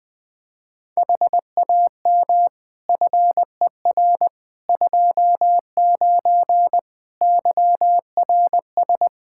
Morse code, heard as a keyed tone, about 20 wpm, 700 Hz. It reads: HAM FER 29 YRS
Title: Practice QSO - Exchange 3 - More Info - 20wpm